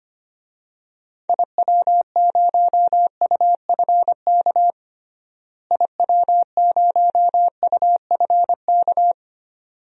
Morse code, heard as a keyed tone, about 25 wpm, 700 Hz. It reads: IW0UFK IW0UFK
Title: DXCC-100 Call Signs - 6 Characters - Repeated 2x - 25wpm